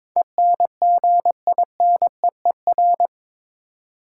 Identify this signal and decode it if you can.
Morse code, heard as a keyed tone, about 21 wpm, 700 Hz. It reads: ENGINEER
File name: practice / 22wpm / Eight Letter Words - 22wpm